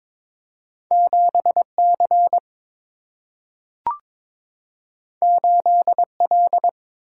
Morse code, heard as a keyed tone, about 22 wpm, 700 Hz. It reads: 7 C 8 L
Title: Single-Letter Single-Number 22wpm